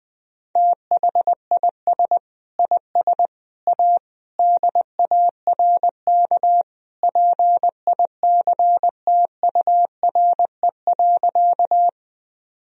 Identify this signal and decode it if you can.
Morse code, heard as a keyed tone, about 20 wpm, 700 Hz. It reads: THIS IS A DARK PICTURE.
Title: Sentences from Top 1000 Words - Review Part 1 - 20wpm